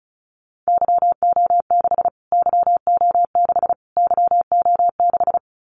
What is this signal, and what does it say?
Morse code, heard as a keyed tone, about 34 wpm, 700 Hz. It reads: YO6 YO6 YO6
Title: Callsign Prefixes for Top 25 DXCC - Repeated 3x - 35wpm